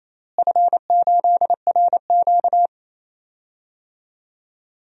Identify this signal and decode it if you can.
Morse code, heard as a keyed tone, about 28 wpm, 700 Hz. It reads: F8RQ
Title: DXCC-100 Call Signs - 4 Characters - Repeated 3x - 28wpm